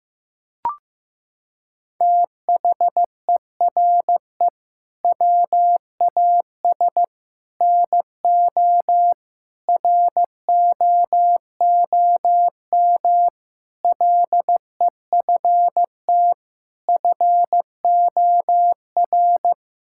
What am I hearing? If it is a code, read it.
Morse code, heard as a keyed tone, about 16 wpm, 700 Hz. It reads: THERE WAS NO ROOM LEFT FOR
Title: Sentences from Top 1600 Words 15wpm